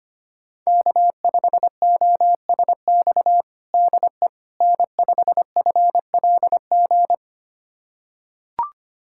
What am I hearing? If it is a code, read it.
Morse code, heard as a keyed tone, about 25 wpm, 700 Hz. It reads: K5OSX DE N5FLG
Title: USA Callsign DE Callsign - 5 Characters - 25wpm